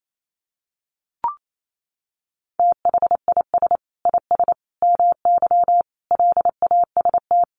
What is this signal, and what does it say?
Morse code, heard as a keyed tone, about 28 wpm, 700 Hz. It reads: THIS IS MY LAST
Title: Sentences from Top 1300 Words 28wpm